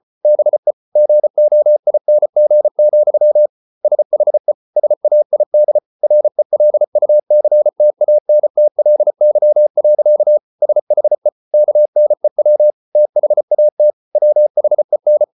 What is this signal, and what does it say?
Morse code, heard as a keyed tone, about 34 wpm, 600 Hz. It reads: BE GOING, SHE SAID RELUCTANTLY. SHE KNEW THAT WHEN